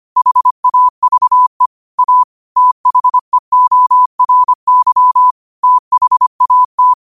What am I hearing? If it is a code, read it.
Morse code, heard as a keyed tone, about 25 wpm, 1000 Hz. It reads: HAVE A THEORY THAT
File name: qotdfiles / 25wpm